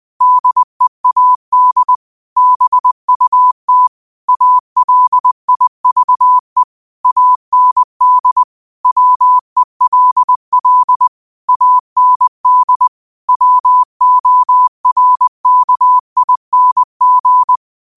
Morse code, heard as a keyed tone, about 20 wpm, 1000 Hz. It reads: DEAD BUT ALIVE AND WELL AND WORKING
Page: Quotes for Thu, 14 Aug 2025 in Morse Code at 20 words per minute.